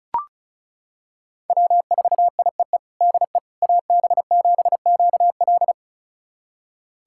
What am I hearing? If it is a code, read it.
Morse code, heard as a keyed tone, about 34 wpm, 700 Hz. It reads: W4IEE DE AB7QL
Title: USA Callsign DE Callsign - 5 Characters - 35wpm